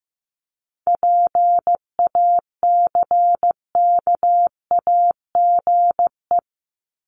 Morse code, heard as a keyed tone, about 15 wpm, 700 Hz. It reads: PACKAGE
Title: Top 1501-1750 Words 15wpm